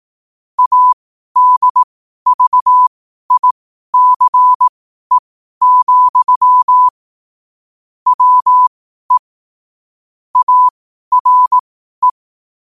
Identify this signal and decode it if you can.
Morse code, heard as a keyed tone, about 18 wpm, spaced as for 12 wpm, 1000 Hz. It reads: ADVICE, WE ARE
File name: qotdfiles / 12wpm